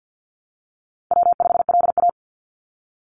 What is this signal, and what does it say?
Morse code, heard as a keyed tone, about 49 wpm, 700 Hz. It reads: W5LA
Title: Call Signs Repeated 3x - 50wpm